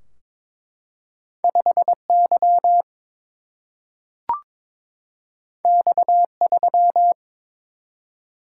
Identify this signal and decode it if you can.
Morse code, heard as a keyed tone, about 22 wpm, 700 Hz. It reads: 5 Y X 3
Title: Single-Letter Single-Number 22wpm